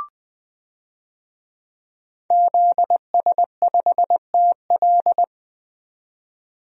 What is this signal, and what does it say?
Morse code, heard as a keyed tone, about 20 wpm, 700 Hz. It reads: ZS5TL
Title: DXCC-100 Call Signs - 5 Characters - 20wpm